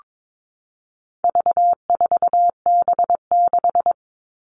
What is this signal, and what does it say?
Morse code, heard as a keyed tone, about 22 wpm, 700 Hz. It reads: V4B6
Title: All Cut Numbers - 4-Digit 22wpm